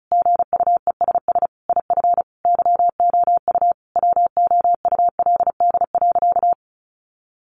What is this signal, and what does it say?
Morse code, heard as a keyed tone, about 34 wpm, 700 Hz. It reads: GUESS IF YOU WOULD.
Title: Sentences from Top 1500 Words 35wpm